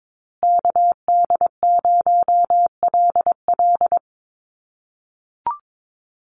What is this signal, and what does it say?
Morse code, heard as a keyed tone, about 22 wpm, 700 Hz. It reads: KD0LL
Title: USA Call Signs Repeated 3x 22wpm